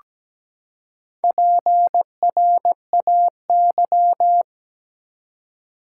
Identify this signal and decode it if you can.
Morse code, heard as a keyed tone, about 17 wpm, 700 Hz. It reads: PRAY
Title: Four Letter Words - 17wpm